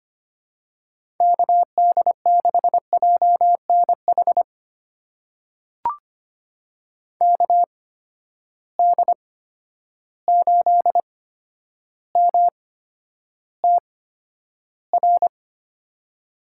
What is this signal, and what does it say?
Morse code, heard as a keyed tone, about 25 wpm, spaced as for 11 wpm, 700 Hz. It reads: KD6JNH KD8MTR
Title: USA Call Signs - 6 Characters - 8x Character Spacing - 25wpm